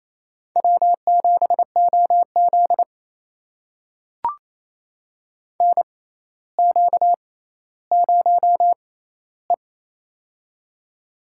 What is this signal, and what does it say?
Morse code, heard as a keyed tone, about 27 wpm, spaced as for 14 wpm, 700 Hz. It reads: W7OZ NQ0E
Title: USA Call Signs - 4 Characters - 6x Character Spacing - 28wpm